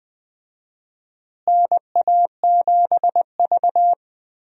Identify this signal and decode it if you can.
Morse code, heard as a keyed tone, about 20 wpm, 700 Hz. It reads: NA7V
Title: USA Call Signs - 4 Characters - Repeated 2x - 20wpm